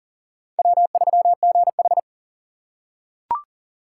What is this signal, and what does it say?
Morse code, heard as a keyed tone, about 37 wpm, 700 Hz. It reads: W3GH
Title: USA Call Signs - 4 Characters - 2x Character Spacing - 40wpm